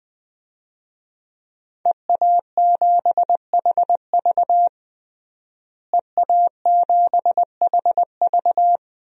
Morse code, heard as a keyed tone, about 21 wpm, 700 Hz. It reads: EA7HV EA7HV
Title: DXCC-100 Call Signs - 5 Characters - Repeated 2x - 20wpm